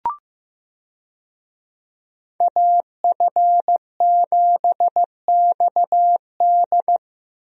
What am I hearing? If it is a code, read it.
Morse code, heard as a keyed tone, about 15 wpm, 700 Hz. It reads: AF7XD
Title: USA Call Signs - 5 Characters - Repeated 2x - 15wpm